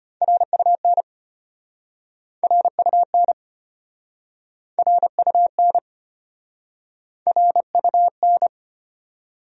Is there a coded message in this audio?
Yes: Morse code from 0.2 to 8.5 s, heard as a keyed tone, about 33 wpm, 700 Hz. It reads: RUN RUN RUN RUN